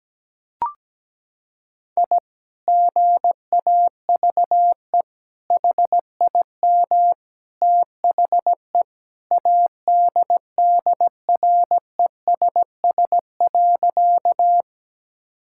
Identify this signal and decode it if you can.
Morse code, heard as a keyed tone, about 17 wpm, 700 Hz. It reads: I GAVE HIM THE ADDRESS.
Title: Sentences from Top 1000 Words - Review Part 6 - 17wpm